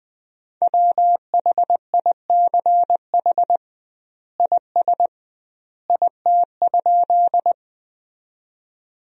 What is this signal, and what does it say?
Morse code, heard as a keyed tone, about 20 wpm, 700 Hz. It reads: WHICH IS IT?
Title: 2x Word-Spacing - Sentences from Top 100 Words - 20wpm